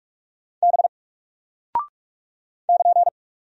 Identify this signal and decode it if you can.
Morse code, heard as a keyed tone, about 44 wpm, 700 Hz. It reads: B <KN>
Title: Single Character (Letter:Number:Punctuation:Prosign) 45wpm